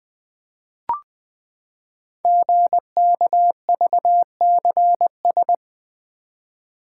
Morse code, heard as a keyed tone, about 20 wpm, 700 Hz. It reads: GKVCS
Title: Code Groups - 5 Letters - 20wpm